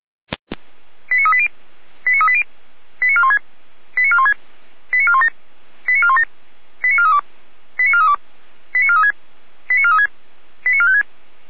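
Five-tone selective call (ZVEI), heard as quick runs of five tones, tones of about 70 ms. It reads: 89380 89380 89416 89416 89417 89417 89432 89432 89436 89436 89456
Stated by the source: ZVEI five-tone call 89192 received by SDR.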